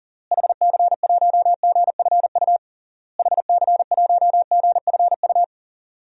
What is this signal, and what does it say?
Morse code, heard as a keyed tone, about 37 wpm, 700 Hz. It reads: HC1GFU HC1GFU
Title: DXCC-100 Call Signs - 6 Characters - Repeated 3x - 40wpm